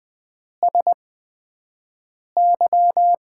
Morse code, heard as a keyed tone, about 20 wpm, spaced as for 5 wpm, 700 Hz. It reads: SY
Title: Top 90 QSO Elements - 8x Character Spacing - 20wpm